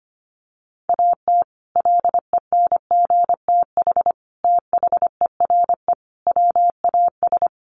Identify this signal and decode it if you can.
Morse code, heard as a keyed tone, about 25 wpm, 700 Hz. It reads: AT LENGTH THERE WAS
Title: Sentences from Top 1000 Words 25wpm